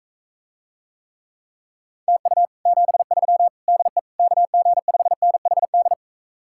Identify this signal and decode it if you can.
Morse code, heard as a keyed tone, about 45 wpm, 700 Hz. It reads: TU 73 DE KG5NHD